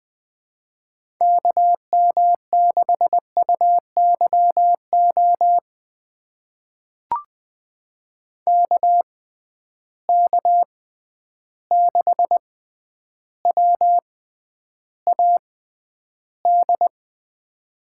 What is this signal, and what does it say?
Morse code, heard as a keyed tone, about 20 wpm, spaced as for 10 wpm, 700 Hz. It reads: KM6UYO KK6WAD
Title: USA Call Signs - 6 Characters - 6x Character Spacing - 20wpm